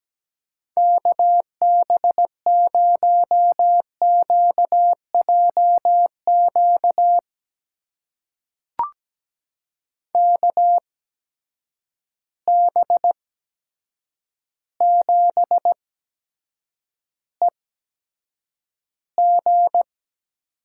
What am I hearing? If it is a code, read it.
Morse code, heard as a keyed tone, about 17 wpm, spaced as for 8 wpm, 700 Hz. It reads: KB0QJQ KB7EG
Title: USA Call Signs - 6 Characters - 8x Character Spacing - 17wpm